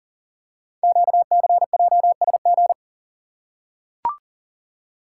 Morse code, heard as a keyed tone, about 39 wpm, 700 Hz. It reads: QCJSG
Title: Code Groups - 5 Letters - 40wpm